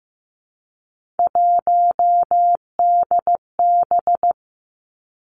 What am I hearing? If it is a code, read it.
Morse code, heard as a keyed tone, about 15 wpm, 700 Hz. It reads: 1DB
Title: Three Challenging Letter-Number Combinations 15wpm